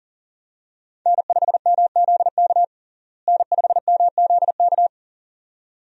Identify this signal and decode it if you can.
Morse code, heard as a keyed tone, about 40 wpm, 700 Hz. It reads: N5MZK N5MZK
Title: USA Call Signs - 5 Characters - Repeated 2x - 40wpm